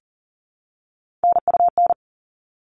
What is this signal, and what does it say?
Morse code, heard as a keyed tone, about 40 wpm, 700 Hz. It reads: NUN
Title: All Cut Numbers - 3-Digit 40wpm